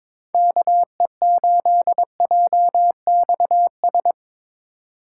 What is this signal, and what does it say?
Morse code, heard as a keyed tone, about 22 wpm, 700 Hz. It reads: KE8JXS
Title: USA Call Signs - 6 Characters - 6x Character Spacing - 22wpm